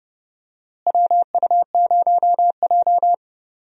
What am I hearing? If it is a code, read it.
Morse code, heard as a keyed tone, about 30 wpm, 700 Hz. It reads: WU0J
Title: USA Call Signs - 4 Characters - Repeated 3x - 30wpm